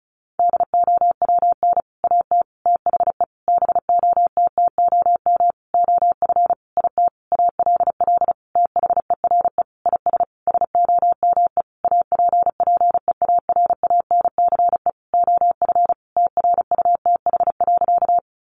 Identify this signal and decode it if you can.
Morse code, heard as a keyed tone, about 34 wpm, 700 Hz. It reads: DOWN AT THE BOTTOM OF IT ALL THERE IS SOME APPEARANCE OF TRUTH.
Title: Sentences from Top 1100 Words 35wpm